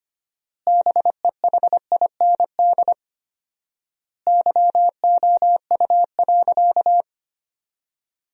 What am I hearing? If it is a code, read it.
Morse code, heard as a keyed tone, about 25 wpm, 700 Hz. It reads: BEHIND YOU.
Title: 4x Word-Spacing - Sentences from Top 500 Words - 25wpm